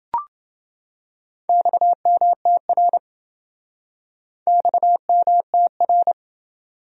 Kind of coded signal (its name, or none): Morse code